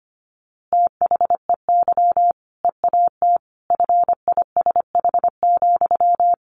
Morse code, heard as a keyed tone, about 25 wpm, 700 Hz. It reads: THEY EAT FISH,
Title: Easy Phrases 25wpm